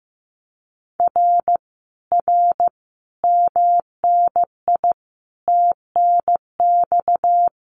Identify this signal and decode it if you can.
Morse code, heard as a keyed tone, about 15 wpm, 700 Hz. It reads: R R MNI TNX
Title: Common QSO Phrases Encore 15wpm